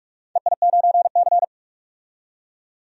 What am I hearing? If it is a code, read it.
Morse code, heard as a keyed tone, about 46 wpm, 700 Hz. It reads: EI9C
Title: DXCC-100 Call Signs - 4 Characters - 45wpm